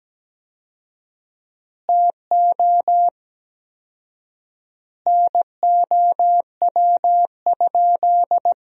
Morse code, heard as a keyed tone, about 17 wpm, 700 Hz. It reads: TO NOW?
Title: 4x Word-Spacing - Sentences from Top 100 Words - 17wpm